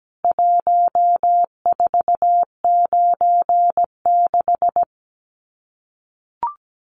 Morse code, heard as a keyed tone, about 17 wpm, 700 Hz. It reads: 1496
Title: CWT Exchange 17wpm